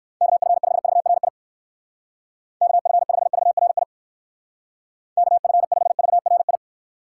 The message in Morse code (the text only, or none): KF5VRI KF5VRI KF5VRI